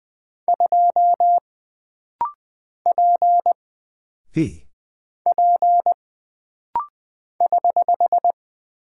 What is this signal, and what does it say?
Morse code, heard as a keyed tone, about 20 wpm, 700 Hz. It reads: 2 P P <HH>
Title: Single Character (Letter:Number:Punctuation:Prosign) - Rapid-Fire - 20wpm